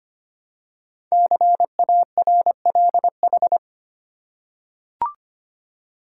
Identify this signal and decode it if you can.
Morse code, heard as a keyed tone, about 25 wpm, 700 Hz. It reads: CARLH
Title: Code Groups - 5 Letters - 25wpm